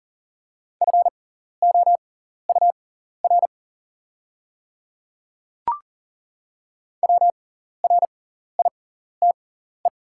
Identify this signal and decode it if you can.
Morse code, heard as a keyed tone, about 40 wpm, spaced as for 14 wpm, 690 Hz. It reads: FOUR WRITE
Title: 6x Character Spacing - Top 300 Words - 40wpm